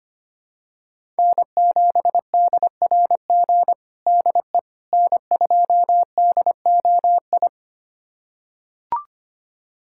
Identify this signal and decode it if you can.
Morse code, heard as a keyed tone, about 25 wpm, 700 Hz. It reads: N7DRG DE N2DOI
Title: USA Callsign DE Callsign - 5 Characters - 25wpm